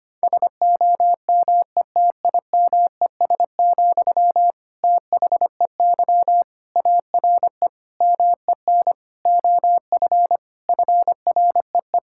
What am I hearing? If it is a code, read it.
Morse code, heard as a keyed tone, about 25 wpm, 700 Hz. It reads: SOMETIMES, THEY ARE MEN OF FREE
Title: Sentences from Top 1000 Words - Review Part 5 - 25wpm